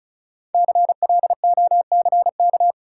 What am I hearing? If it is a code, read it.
Morse code, heard as a keyed tone, about 36 wpm, 700 Hz. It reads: CLOCK
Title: Five Letter Words - 35wpm